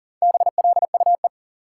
Morse code, heard as a keyed tone, about 40 wpm, 700 Hz. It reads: BLUE
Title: Sentences from Top 1000 Words - Review Part 3 - New Word Drill - 40wpm